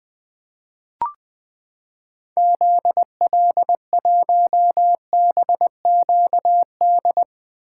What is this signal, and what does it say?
Morse code, heard as a keyed tone, about 20 wpm, 700 Hz. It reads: ZL1BQD
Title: 2023 POTA Activator Call Signs - 20wpm